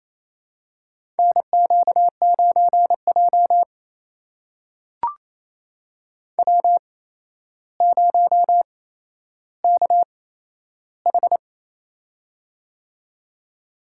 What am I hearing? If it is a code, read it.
Morse code, heard as a keyed tone, about 27 wpm, spaced as for 12 wpm, 700 Hz. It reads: NQ9J W0KH
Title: USA Call Signs - 4 Characters - 8x Character Spacing - 28wpm